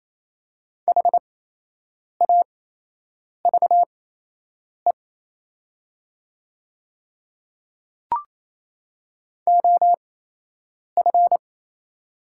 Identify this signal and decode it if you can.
Morse code, heard as a keyed tone, about 28 wpm, spaced as for 8 wpm, 700 Hz. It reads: HAVE OF
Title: 8x Character Spacing - Top 100 Words - 28wpm